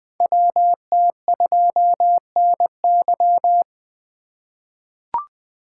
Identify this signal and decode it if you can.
Morse code, heard as a keyed tone, about 20 wpm, 700 Hz. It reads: WT2NY
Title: USA Call Signs - 3x Character Spacing - 20wpm